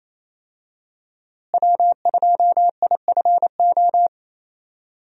Morse code, heard as a keyed tone, about 28 wpm, 700 Hz. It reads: W2IFO
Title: USA Call Signs - 5 Characters - Repeated 2x - 28wpm